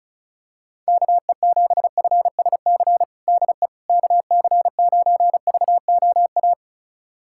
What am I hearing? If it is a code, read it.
Morse code, heard as a keyed tone, about 36 wpm, 690 Hz. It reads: KE7FSC DE KC9VOA
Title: USA Callsign DE Callsign - 6 Characters - 35wpm